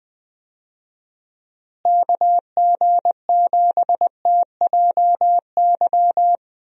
Morse code, heard as a keyed tone, about 20 wpm, 700 Hz. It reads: KG7TJY
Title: USA Call Signs - 6 Characters - Repeated 3x - 20wpm